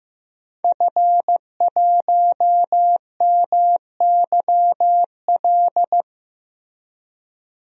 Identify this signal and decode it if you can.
Morse code, heard as a keyed tone, about 15 wpm, 700 Hz. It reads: F1MYL
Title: DXCC-100 Call Signs - 5 Characters - Repeated 3x - 15wpm